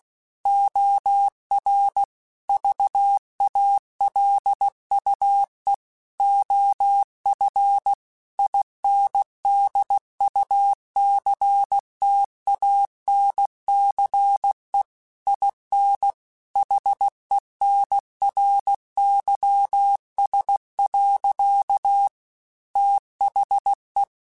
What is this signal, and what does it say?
Morse code, heard as a keyed tone, about 16 wpm, 800 Hz. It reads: OR VALUE OF INDUCTANCE IN HENRYS. THE
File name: Technical Subjects / Technical Subjects-mp3s / XL @15